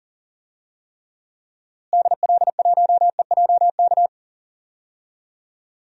Morse code, heard as a keyed tone, about 40 wpm, 700 Hz. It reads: DL1EJK
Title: DXCC-100 Call Signs - 6 Characters - Repeated 3x - 40wpm